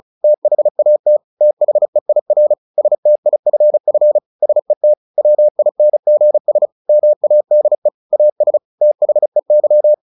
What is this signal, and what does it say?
Morse code, heard as a keyed tone, about 34 wpm, 600 Hz. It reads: THAT THEIR STIFF SET WINGS MADE AS THEY
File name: the old man and the sea / the old man and the sea 1 35wpm